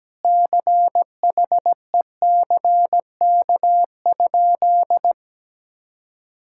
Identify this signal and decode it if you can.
Morse code, heard as a keyed tone, about 17 wpm, 700 Hz. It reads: CHECK?
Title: Sentences from Top 1000 Words - Review Part 6 - 17wpm